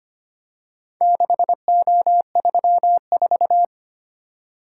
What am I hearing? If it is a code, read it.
Morse code, heard as a keyed tone, about 25 wpm, 700 Hz. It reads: 6O34
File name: practice / 25wpm / Four Letter-Number Combinations - 25wpm